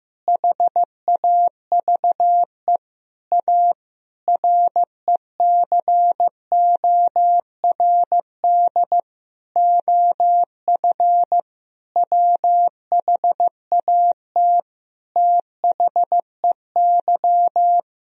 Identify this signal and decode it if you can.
Morse code, heard as a keyed tone, about 16 wpm, 700 Hz. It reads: HAVE A RECORD OF WHAT THEY
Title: Sentences from Top 1000 Words - Review Part 3 - 15wpm